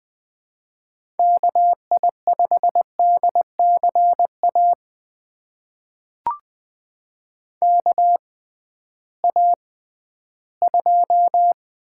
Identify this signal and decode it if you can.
Morse code, heard as a keyed tone, about 20 wpm, spaced as for 13 wpm, 700 Hz. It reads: KI5DCA KA2
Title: USA Call Signs - 6 Characters - 6x Character Spacing - 20wpm